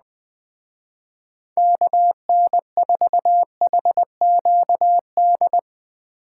Morse code, heard as a keyed tone, about 21 wpm, 700 Hz. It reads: KN4HQD